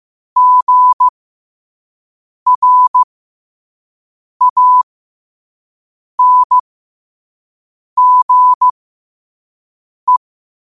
Morse code, heard as a keyed tone, about 15 wpm, spaced as for 5 wpm, 1000 Hz. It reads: GRANGE
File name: qotdfiles / 5wpm